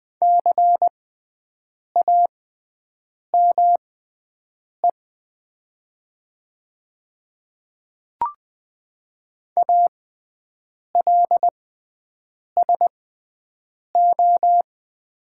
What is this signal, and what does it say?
Morse code, heard as a keyed tone, about 20 wpm, spaced as for 7 wpm, 700 Hz. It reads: CAME ALSO
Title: 6x Character Spacing - Top 200 Words - 20wpm